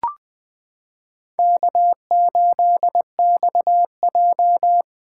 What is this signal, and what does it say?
Morse code, heard as a keyed tone, about 20 wpm, 700 Hz. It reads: K8XJ
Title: K1USN SST QSO Practice - 20wpm